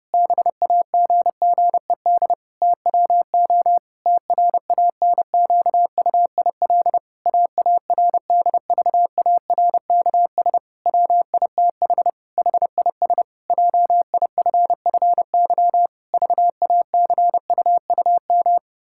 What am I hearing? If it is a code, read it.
Morse code, heard as a keyed tone, about 30 wpm, 700 Hz. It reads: BAGGED TWO TRANQUIL AARDVARKS WITH HIS JIFFY VACUUM
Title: Pangram Challenge v2 - 30wpm